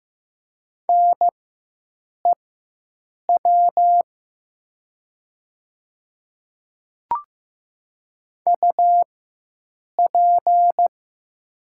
Morse code, heard as a keyed tone, about 15 wpm, spaced as for 7 wpm, 700 Hz. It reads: NEW UP
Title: 4x Character Spacing - Top 500 Words - 15wpm